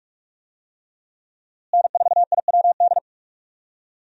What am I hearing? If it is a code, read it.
Morse code, heard as a keyed tone, about 46 wpm, 700 Hz. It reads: N4IWD
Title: USA Call Signs - 5 Characters - Repeated 3x - 45wpm